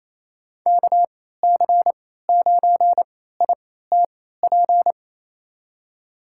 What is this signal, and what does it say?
Morse code, heard as a keyed tone, about 28 wpm, spaced as for 16 wpm, 700 Hz. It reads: KC9ITP
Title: USA Call Signs - 6 Characters - 3x Character Spacing - 28wpm